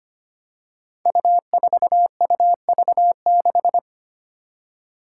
Morse code, heard as a keyed tone, about 25 wpm, 700 Hz. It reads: U4UV6
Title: Five Challenging Letter-Number Combinations 25wpm